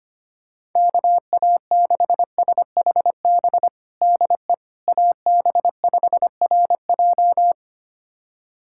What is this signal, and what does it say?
Morse code, heard as a keyed tone, about 25 wpm, 700 Hz. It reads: KA6SHB DE AB5RJ
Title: USA Callsign DE Callsign 25wpm